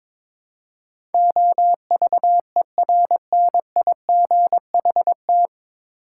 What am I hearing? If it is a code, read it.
Morse code, heard as a keyed tone, about 22 wpm, 700 Hz. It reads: OVERNIGHT
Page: Morse Code Ninja practice 22wpm